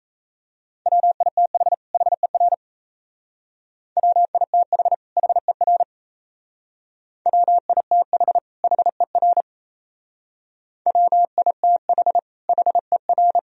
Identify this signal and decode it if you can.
Morse code, heard as a keyed tone, about 36 wpm, 700 Hz. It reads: WITH HER WITH HER WITH HER WITH HER